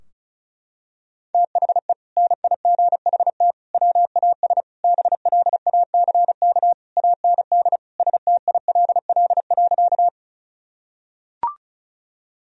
Morse code, heard as a keyed tone, about 34 wpm, 700 Hz. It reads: THE NIGHT WAS BLACK AND STILL.
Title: Sentences from Top 1000 Words - Review Part 2 - 35wpm